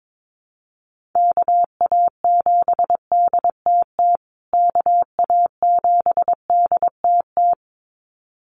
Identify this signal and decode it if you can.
Morse code, heard as a keyed tone, about 22 wpm, 700 Hz. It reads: KA7DTT KA7DTT
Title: USA Call Signs Repeated 2x 22wpm